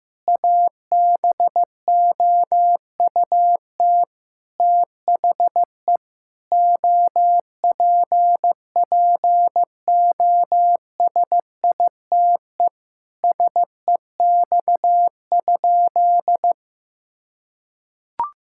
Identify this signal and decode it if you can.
Morse code, heard as a keyed tone, about 15 wpm, 700 Hz. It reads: ABOUT THE OPPOSITE SEX?
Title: Sentences from Top 1600 Words 15wpm